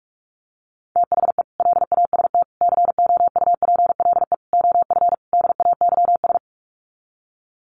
Morse code, heard as a keyed tone, about 44 wpm, 700 Hz. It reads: THE LAST COUPLE OF DAYS
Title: Sets of 5 Words Encore 45wpm